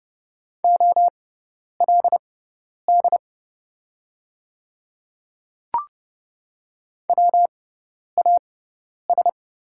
Morse code, heard as a keyed tone, about 29 wpm, spaced as for 10 wpm, 690 Hz. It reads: OLD WAS
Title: 6x Character Spacing - Top 300 Words - 30wpm